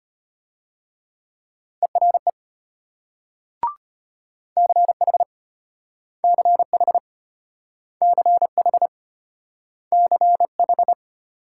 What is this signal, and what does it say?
Morse code, heard as a keyed tone, about 33 wpm, 700 Hz. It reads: ERE CH CH CH CH